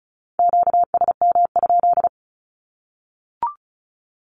Morse code, heard as a keyed tone, about 34 wpm, 700 Hz. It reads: QSM?
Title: Q-Codes 35wpm